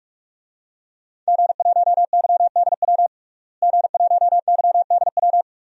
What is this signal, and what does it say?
Morse code, heard as a keyed tone, about 46 wpm, 700 Hz. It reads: G1YDW G1YDW
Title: DXCC-100 Call Signs - 5 Characters - Repeated 2x - 45wpm